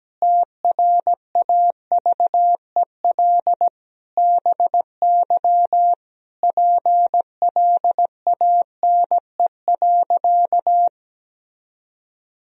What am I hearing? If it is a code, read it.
Morse code, heard as a keyed tone, about 17 wpm, 700 Hz. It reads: TRAVEL BY PLANE.
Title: Sentences from Top 1000 Words - Review Part 3 - 17wpm